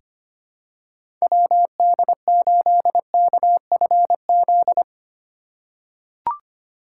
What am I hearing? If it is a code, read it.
Morse code, heard as a keyed tone, about 25 wpm, 700 Hz. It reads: WD8KFZ
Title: USA Call Signs - 6 Characters - 4x Character Spacing - 25wpm